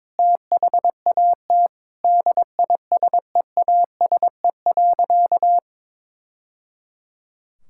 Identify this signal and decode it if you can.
Morse code, heard as a keyed tone, about 22 wpm, 700 Hz. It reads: THAT DISEASE.
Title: Sentences from Top 1000 Words - Review Part 6 - 22wpm